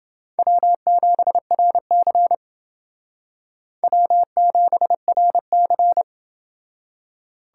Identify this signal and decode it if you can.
Morse code, heard as a keyed tone, about 28 wpm, 700 Hz. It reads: W7RC W7RC